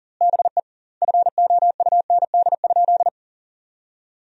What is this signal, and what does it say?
Morse code, heard as a keyed tone, about 40 wpm, 700 Hz. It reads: BE FOUND?
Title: 2x Word-Spacing - Sentences from Top 300 Words - 40wpm